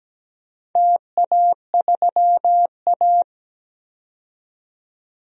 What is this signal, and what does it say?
Morse code, heard as a keyed tone, about 17 wpm, 700 Hz. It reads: TA3A
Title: DXCC-100 Call Signs - 4 Characters - 17wpm